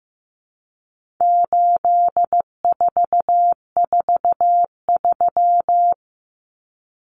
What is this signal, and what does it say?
Morse code, heard as a keyed tone, about 15 wpm, 700 Hz. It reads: 8443
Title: Common Cut Numbers - 4-Digit 15wpm